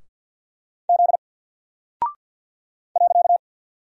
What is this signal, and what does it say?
Morse code, heard as a keyed tone, about 48 wpm, 700 Hz. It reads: Z .
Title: Single Character (Letter:Number:Punctuation:Prosign) 50wpm